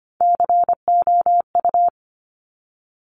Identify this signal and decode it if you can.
Morse code, heard as a keyed tone, about 25 wpm, 700 Hz. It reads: COU
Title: Top Three Letter Combinations in Words 25wpm